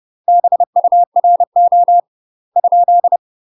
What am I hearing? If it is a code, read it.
Morse code, heard as a keyed tone, about 30 wpm, 700 Hz. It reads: BURO ?
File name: UK MASTER QSOs TRAINING / 30WPM / UK MASTER QSOs 0045